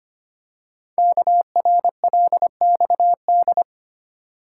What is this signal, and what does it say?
Morse code, heard as a keyed tone, about 25 wpm, 700 Hz. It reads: KRLXD
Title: Code Groups - 5 Letters - 25wpm